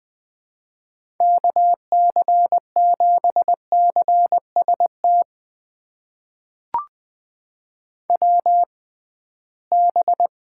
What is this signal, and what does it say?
Morse code, heard as a keyed tone, about 20 wpm, spaced as for 15 wpm, 700 Hz. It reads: KC7CST WB
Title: USA Call Signs - 6 Characters - 6x Character Spacing - 20wpm